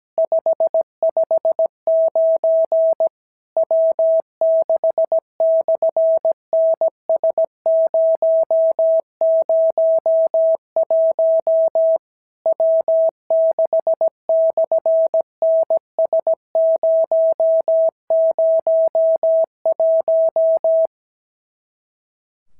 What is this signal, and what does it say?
Morse code, heard as a keyed tone, about 17 wpm, 650 Hz. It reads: 559 W6/NS001 W6/NS001
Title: SOTA QSO Practice 17wpm